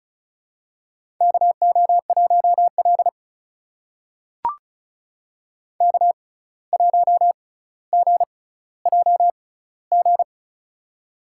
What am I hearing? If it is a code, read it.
Morse code, heard as a keyed tone, about 34 wpm, spaced as for 17 wpm, 700 Hz. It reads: KO1L K1GJG
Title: USA Call Signs - 6x Character Spacing - 35wpm